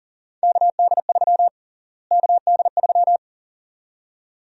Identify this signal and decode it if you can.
Morse code, heard as a keyed tone, about 40 wpm, 700 Hz. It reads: KD3 KD3
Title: USA Call Sign Prefixes - Repeated 2x - 40wpm